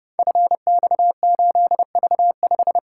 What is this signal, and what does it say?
Morse code, heard as a keyed tone, about 30 wpm, 700 Hz. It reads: FX8V5
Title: Code Groups - 5 Letter:Numbers - 30wpm